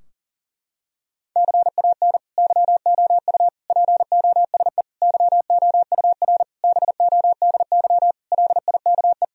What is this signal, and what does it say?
Morse code, heard as a keyed tone, about 40 wpm, 700 Hz. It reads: CAN YOU POSE YOUR BODY LIKE
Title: Sentences from Top 1000 Words - Review Part 2 - 40wpm